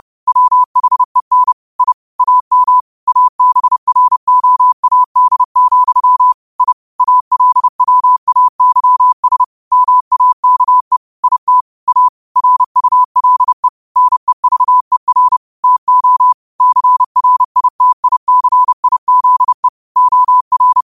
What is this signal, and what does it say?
Morse code, heard as a keyed tone, about 30 wpm, 1000 Hz. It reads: WHEN I AM ABROAD, I ALWAYS MAKE IT A RULE NEVER TO CRITICIZE OR
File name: qotdfiles / 30wpm